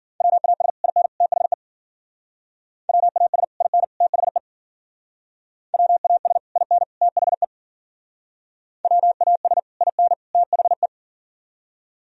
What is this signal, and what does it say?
Morse code, heard as a keyed tone, about 50 wpm, 700 Hz. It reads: WAS IN THE WAS IN THE WAS IN THE WAS IN THE